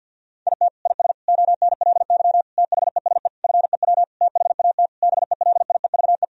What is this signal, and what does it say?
Morse code, heard as a keyed tone, about 52 wpm, 700 Hz. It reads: IT IS ONLY THESE FEW THAT BELIEVE
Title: Sentences from Top 1000 Words - Review Part 1 - 50wpm